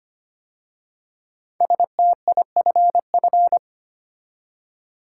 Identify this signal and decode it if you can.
Morse code, heard as a keyed tone, about 25 wpm, 700 Hz. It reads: STIFF
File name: 5 Letter Words - First 3 Letters Given - 25wpm